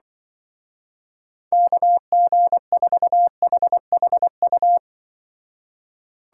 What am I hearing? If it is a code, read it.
Morse code, heard as a keyed tone, about 24 wpm, 700 Hz. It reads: KG4HHU
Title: Speed Racing - USA 2x3 Call Signs 20wpm